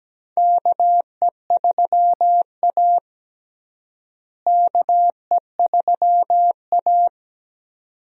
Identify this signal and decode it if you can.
Morse code, heard as a keyed tone, about 17 wpm, 700 Hz. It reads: KE3A KE3A
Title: USA Call Signs - 4 Characters - Repeated 2x - 17wpm